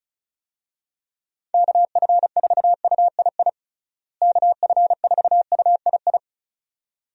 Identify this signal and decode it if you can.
Morse code, heard as a keyed tone, about 36 wpm, 700 Hz. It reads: KF4UII KF4UII
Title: USA Call Signs - 6 Characters - Repeated 2x - 35wpm